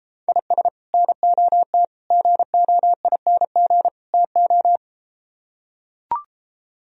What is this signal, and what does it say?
Morse code, heard as a keyed tone, about 34 wpm, 700 Hz. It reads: IS NOT GOING TO